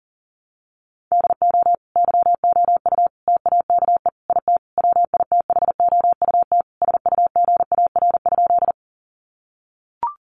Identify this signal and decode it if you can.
Morse code, heard as a keyed tone, about 37 wpm, 700 Hz. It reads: DO YOU TAKE IT WITHOUT SUGAR?
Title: Sentences from Top 1200 Words 40wpm